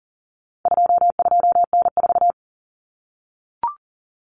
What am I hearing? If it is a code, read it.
Morse code, heard as a keyed tone, about 40 wpm, 700 Hz. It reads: 22N4
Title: Common Cut Numbers - 4-Digit 40wpm